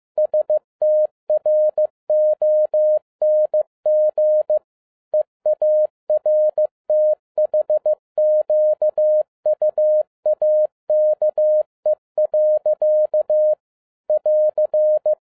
Morse code, heard as a keyed tone, about 15 wpm, 610 Hz. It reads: STRONG EARTHQUAKE. <AR>
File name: FIELD DAY CONTEST QRV FROM SAKAE VILLAGE 15wpm